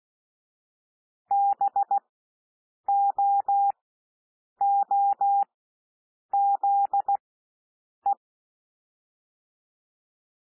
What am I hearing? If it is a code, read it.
Morse code, heard as a keyed tone, about 16 wpm, spaced as for 7 wpm, 800 Hz. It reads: BOOZE
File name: SOUND WORDS 16 wpm